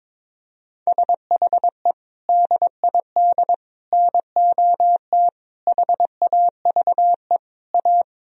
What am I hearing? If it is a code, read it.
Morse code, heard as a keyed tone, about 22 wpm, 700 Hz. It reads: SHE DID NOT HAVE A
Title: Sentences from Top 1000 Words - Review Part 2 - 22wpm